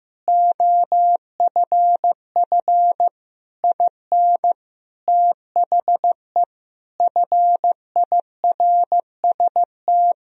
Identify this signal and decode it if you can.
Morse code, heard as a keyed tone, about 15 wpm, 700 Hz. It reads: OFF IN THE FIRST
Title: Sentences from Top 200 Words and Abbreviations 15wpm